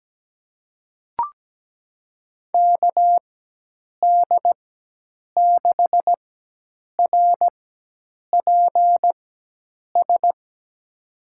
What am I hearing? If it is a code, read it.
Morse code, heard as a keyed tone, about 17 wpm, spaced as for 8 wpm, 700 Hz. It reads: KD6RPS
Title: USA Call Signs - 4x Character Spacing - 17wpm